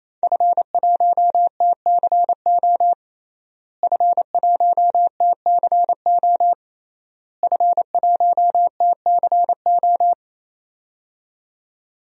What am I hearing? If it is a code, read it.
Morse code, heard as a keyed tone, about 28 wpm, 700 Hz. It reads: F1TCO F1TCO F1TCO
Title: DXCC-100 Call Signs - 5 Characters - Repeated 3x - 28wpm